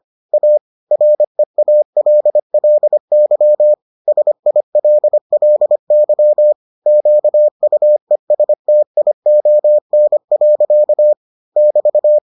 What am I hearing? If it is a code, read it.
Morse code, heard as a keyed tone, about 25 wpm, 600 Hz. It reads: A REALLY SILLY QUESTION. <BT>